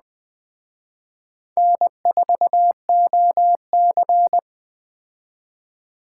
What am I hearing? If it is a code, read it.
Morse code, heard as a keyed tone, about 20 wpm, 700 Hz. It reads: N4OC
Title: Speed Racing - USA 1x2 Call Signs 17wpm